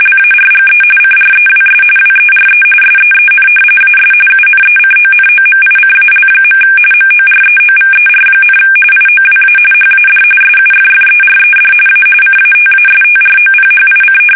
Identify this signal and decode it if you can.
RTTY, about 45.45 baud. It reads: LARGE CREWS WERE SEEN AS UNECONOMICAL. MOREOVER, IOWA HAD SUFFERED A CATASTROPHIC
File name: fsk850 / battleshipnj